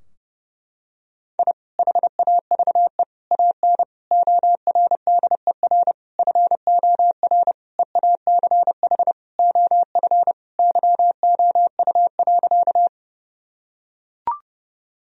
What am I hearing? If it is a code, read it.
Morse code, heard as a keyed tone, about 30 wpm, 700 Hz. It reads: I HAVE AN ORDER FOR EACH OF YOU.
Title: Sentences from Top 1000 Words - Review Part 2 - 30wpm